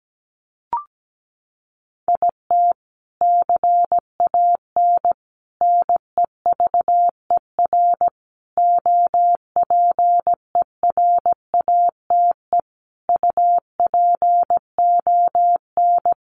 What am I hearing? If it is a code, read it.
Morse code, heard as a keyed tone, about 17 wpm, 700 Hz. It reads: IT CAN NEVER OPERATE UPON
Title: Sentences from Top 1500 Words 17wpm